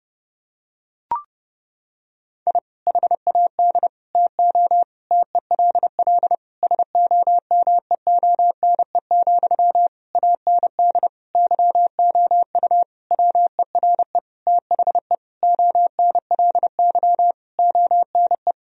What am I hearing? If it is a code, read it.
Morse code, heard as a keyed tone, about 30 wpm, 700 Hz. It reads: I HAD TO TELL SOMEONE, AND YOU WERE THE ONLY ONE
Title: Sentences from Top 1000 Words - Review Part 4 - 30wpm